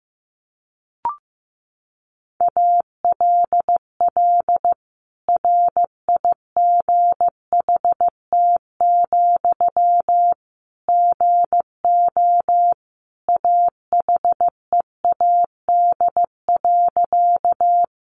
Morse code, heard as a keyed tone, about 15 wpm, 700 Hz. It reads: ALL RIGHT, GO AHEAD.
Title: Sentences from Top 1100 Words 15wpm